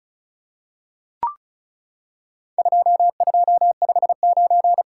Morse code, heard as a keyed tone, about 36 wpm, 700 Hz. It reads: 2259
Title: Worked All Europe DX Contest - QTC Exchange - 35wpm